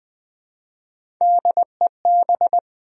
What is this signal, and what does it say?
Morse code, heard as a keyed tone, about 20 wpm, 700 Hz. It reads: DEB
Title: USA Common Names - Max 3 Letters - 20wpm